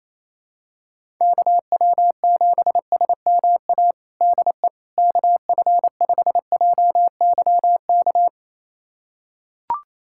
Morse code, heard as a keyed tone, about 28 wpm, 700 Hz. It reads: KW7SMA DE KF5JYK
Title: USA Callsign DE Callsign 28wpm